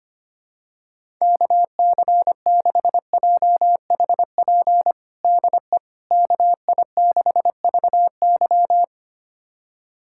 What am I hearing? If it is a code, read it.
Morse code, heard as a keyed tone, about 25 wpm, 700 Hz. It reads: KC6JHP DE KI6VY
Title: USA Callsign DE Callsign 25wpm